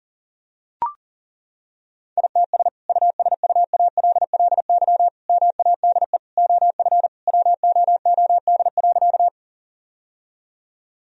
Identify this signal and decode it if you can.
Morse code, heard as a keyed tone, about 40 wpm, 700 Hz. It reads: ITS USUALLY MADE OF WOOD.
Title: Sentences from Top 1000 Words - Review Part 3 - 40wpm